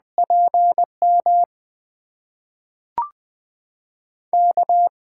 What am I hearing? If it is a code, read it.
Morse code, heard as a keyed tone, about 21 wpm, 700 Hz. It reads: PM K
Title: USA Call Signs - 6 Characters - 8x Character Spacing - 20wpm